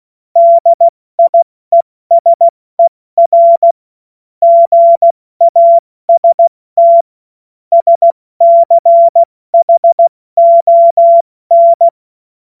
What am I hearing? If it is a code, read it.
Morse code, heard as a keyed tone, about 16 wpm, 680 Hz. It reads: DIESER GAST SCHON
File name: Witz-16-September-2025-70-F80